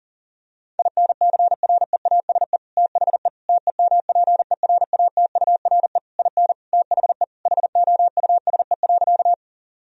Morse code, heard as a keyed tone, about 40 wpm, 700 Hz. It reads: INCREASE THE TEMPERATURE IN THE HOUSE.
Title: Sentences from Top 1000 Words - Review Part 4 - 40wpm